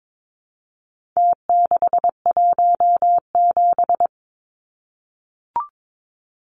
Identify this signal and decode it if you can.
Morse code, heard as a keyed tone, about 22 wpm, 700 Hz. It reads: T617
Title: Numbers - Contest Style - 4-Digit 22wpm